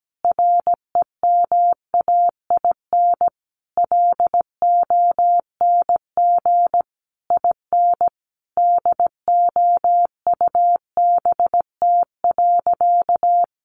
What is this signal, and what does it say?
Morse code, heard as a keyed tone, about 17 wpm, 700 Hz. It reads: REMAIN LONG IN DOUBT.
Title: Sentences from Top 1500 Words 17wpm